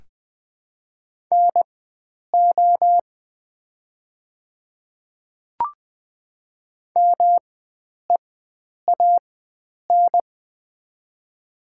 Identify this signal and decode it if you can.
Morse code, heard as a keyed tone, about 21 wpm, spaced as for 9 wpm, 700 Hz. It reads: NO MEAN
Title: 4x Character Spacing - Top 300 Words - 20wpm